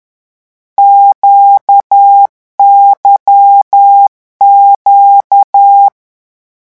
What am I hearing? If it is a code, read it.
Morse code, heard as a keyed tone, about 10 wpm, 800 Hz. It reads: QYQ
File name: morse mp3 / lecon2-vitesse 10